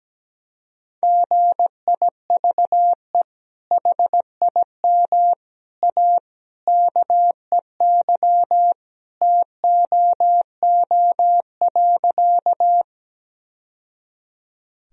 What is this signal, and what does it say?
Morse code, heard as a keyed tone, about 17 wpm, 700 Hz. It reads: GIVE HIM A KEY TOO.
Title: Sentences from Top 1000 Words - Review Part 4 - 17wpm